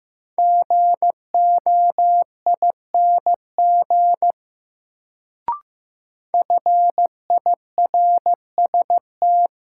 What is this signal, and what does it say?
Morse code, heard as a keyed tone, about 15 wpm, 700 Hz. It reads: GOING FIRST
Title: Sets of 2 Words - New Word Drill - 15wpm